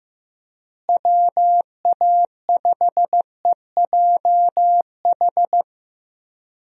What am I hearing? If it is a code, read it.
Morse code, heard as a keyed tone, about 15 wpm, 700 Hz. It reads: WA5EJH
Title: USA Call Signs - 6 Characters - Repeated 2x - 15wpm